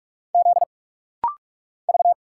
Morse code, heard as a keyed tone, about 45 wpm, 700 Hz. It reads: Z V
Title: Single Letter-Number Alt - Rapid-Fire 45wpm